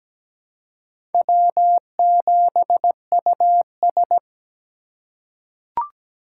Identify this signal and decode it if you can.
Morse code, heard as a keyed tone, about 17 wpm, 700 Hz. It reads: W7US
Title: USA Call Signs - 4 Characters - 4x Character Spacing - 17wpm